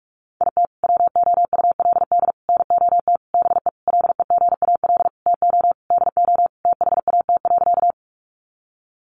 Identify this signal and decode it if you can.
Morse code, heard as a keyed tone, about 44 wpm, 700 Hz. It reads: IT WOULD NOT BE LEGAL TO DO THAT.
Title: Sentences from Top 1200 Words 45wpm